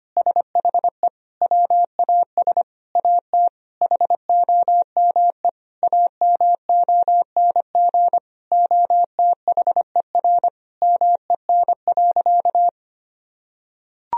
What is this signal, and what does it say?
Morse code, heard as a keyed tone, about 25 wpm, 700 Hz. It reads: SHE WAS AT HOME AMONG OTHER MEN.
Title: Sentences from Top 1000 Words - Review Part 3 - 25wpm